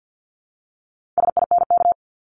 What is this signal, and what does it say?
Morse code, heard as a keyed tone, about 50 wpm, 700 Hz. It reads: SINK
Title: Multi-letter Phonemes Repeated 3x 50wpm